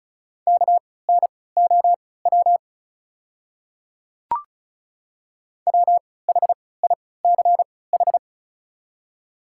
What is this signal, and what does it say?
Morse code, heard as a keyed tone, about 36 wpm, spaced as for 20 wpm, 700 Hz. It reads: KNOW WHICH
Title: 3x Character Spacing - Top 100 Words - 35wpm